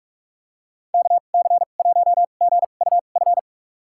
Morse code, heard as a keyed tone, about 46 wpm, spaced as for 32 wpm, 700 Hz. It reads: KC1GUF
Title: USA Call Signs - 2x Character Spacing - 45wpm